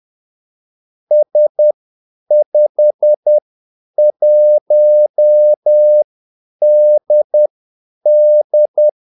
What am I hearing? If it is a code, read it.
Morse code, heard as a keyed tone, about 10 wpm, spaced as for 8 wpm, 600 Hz. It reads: S51DD
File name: Callsigns-10-8wpm